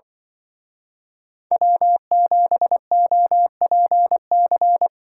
Morse code, heard as a keyed tone, about 24 wpm, 700 Hz. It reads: W7OPC
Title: Speed Racing - USA 1x3 Call Signs 20wpm